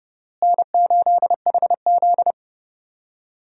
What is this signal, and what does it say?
Morse code, heard as a keyed tone, about 30 wpm, 700 Hz. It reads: N8HZ
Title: USA Call Signs - 4 Characters - Repeated 3x - 30wpm